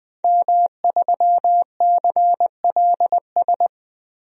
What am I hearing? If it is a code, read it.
Morse code, heard as a keyed tone, about 20 wpm, 700 Hz. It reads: M3CLS
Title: DXCC-100 Call Signs - 5 Characters - 20wpm